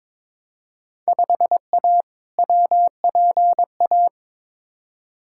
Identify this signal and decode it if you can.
Morse code, heard as a keyed tone, about 21 wpm, 700 Hz. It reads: 5A WPA
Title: Field Day Exchanges Encore - 22wpm